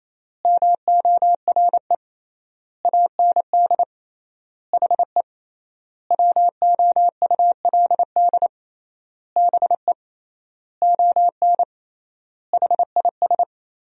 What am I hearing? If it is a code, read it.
Morse code, heard as a keyed tone, about 28 wpm, 700 Hz. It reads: MORE AND HE WOULD BE ON HIS
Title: 3x Word-Spacing - Sentences from Top 300 Words - 28wpm